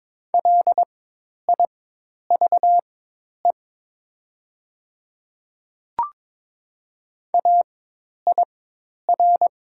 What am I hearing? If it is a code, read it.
Morse code, heard as a keyed tone, about 22 wpm, spaced as for 10 wpm, 700 Hz. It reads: LIVE AIR
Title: 4x Character Spacing - Top 300 Words - 22wpm